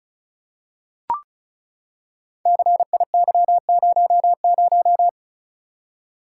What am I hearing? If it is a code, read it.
Morse code, heard as a keyed tone, about 35 wpm, 700 Hz. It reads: CIY00
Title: Code Groups - 5 Letter:Numbers - 35wpm